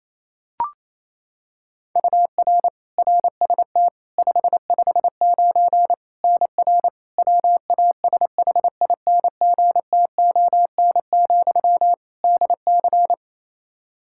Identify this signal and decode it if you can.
Morse code, heard as a keyed tone, about 28 wpm, 700 Hz. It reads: UR RST 559 NR WASHINGTON, DC
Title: K1USN SST QSO Practice 28wpm